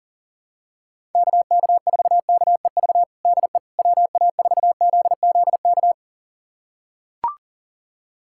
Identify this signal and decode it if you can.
Morse code, heard as a keyed tone, about 40 wpm, 700 Hz. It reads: KK4KEV DE WA4ZZK
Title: USA Callsign DE Callsign - 6 Characters - 40wpm